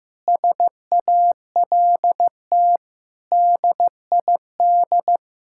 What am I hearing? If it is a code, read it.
Morse code, heard as a keyed tone, about 15 wpm, 700 Hz. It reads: SALT DID
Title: Sentences from Top 1000 Words - Review Part 5 - 15wpm